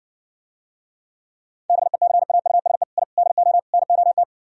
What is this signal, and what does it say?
Morse code, heard as a keyed tone, about 70 wpm, 700 Hz. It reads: BECAUSE I DO NOT